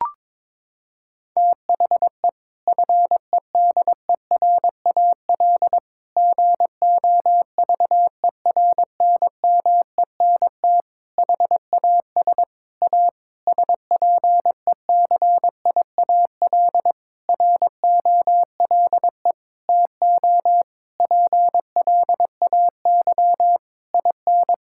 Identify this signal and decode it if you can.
Morse code, heard as a keyed tone, about 22 wpm, 700 Hz. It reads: THE FEDERAL GOVERNMENT HAS A SPECIAL ROLE TO PLAY IN
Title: Sentences from Top 1000 Words - Review Part 6 - 22wpm